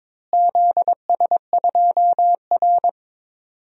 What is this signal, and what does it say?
Morse code, heard as a keyed tone, about 22 wpm, 700 Hz. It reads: ZS2R
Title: DXCC-100 Call Signs - 4 Characters - Repeated 3x - 22wpm